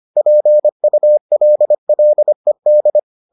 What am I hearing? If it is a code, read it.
Morse code, heard as a keyed tone, about 25 wpm, 600 Hz. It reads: PULLED